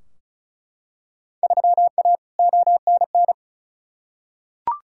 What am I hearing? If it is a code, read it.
Morse code, heard as a keyed tone, about 36 wpm, 690 Hz. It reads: 3A ONN
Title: Field Day Exchanges Encore - 35wpm